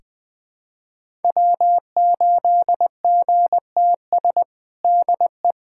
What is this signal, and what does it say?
Morse code, heard as a keyed tone, about 21 wpm, 700 Hz. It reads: W8GTS DE
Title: USA Callsign DE Callsign - 5 Characters - 20wpm